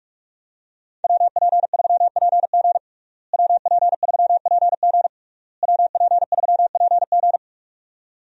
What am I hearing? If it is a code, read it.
Morse code, heard as a keyed tone, about 46 wpm, 700 Hz. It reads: WP3PG WP3PG WP3PG
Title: USA Call Signs - 5 Characters - Repeated 3x - 45wpm